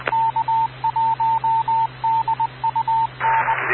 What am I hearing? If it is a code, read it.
Morse code, heard as a keyed tone, about 20 wpm, 900 Hz, with radio noise. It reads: K1DUN
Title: Rocky Mountain Radio League Repeaters